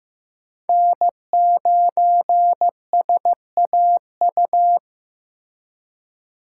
Morse code, heard as a keyed tone, about 18 wpm, 700 Hz. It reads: N9SAU
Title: Speed Racing - USA Call Signs 15wpm